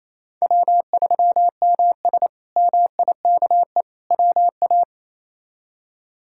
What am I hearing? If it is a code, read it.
Morse code, heard as a keyed tone, about 28 wpm, 700 Hz. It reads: W3MS MIKE WA
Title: K1USN SST QSO Practice - 28wpm